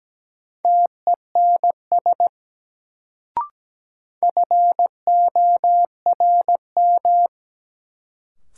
Morse code, heard as a keyed tone, about 17 wpm, 700 Hz. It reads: TENS FORM
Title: Sets of 4 Words - New Word Drill - 17wpm